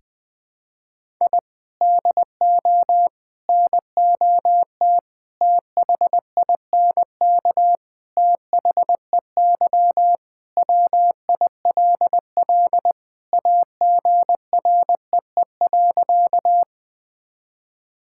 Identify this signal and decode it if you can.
Morse code, heard as a keyed tone, about 20 wpm, 700 Hz. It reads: I DO NOT THINK THEY WILL AGREE.
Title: Sentences with 5-Word Phrases - 20wpm